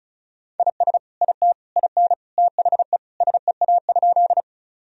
Morse code, heard as a keyed tone, about 36 wpm, 700 Hz. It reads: IS IT IN THE SEA?
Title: Sentences from Top 1000 Words - Review Part 1 - 35wpm